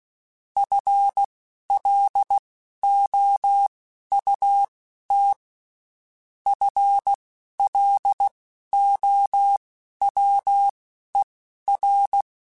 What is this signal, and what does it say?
Morse code, heard as a keyed tone, about 16 wpm, spaced as for 11 wpm, 800 Hz. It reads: FLOUT FLOWER
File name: Spaced / FL Runs sp mp3s / FL Runs sp @15